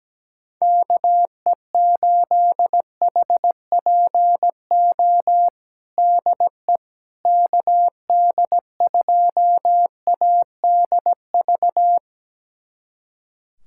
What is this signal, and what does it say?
Morse code, heard as a keyed tone, about 17 wpm, 700 Hz. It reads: KE8HPO DE KD2ADV
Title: USA Callsign DE Callsign - 6 Characters - 17wpm